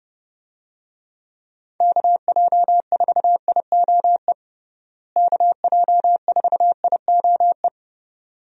Morse code, heard as a keyed tone, about 30 wpm, 700 Hz. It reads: KJ4IOE KJ4IOE
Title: USA Call Signs - 6 Characters - Repeated 2x - 30wpm